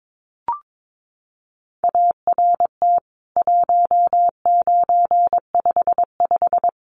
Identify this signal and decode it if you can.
Morse code, heard as a keyed tone, about 22 wpm, 700 Hz. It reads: ART 1955
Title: CWT Exchange 22wpm